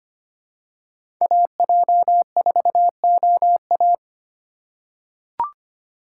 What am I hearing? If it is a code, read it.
Morse code, heard as a keyed tone, about 25 wpm, 700 Hz. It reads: AJ4OA
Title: USA Call Signs - 3x Character Spacing - 25wpm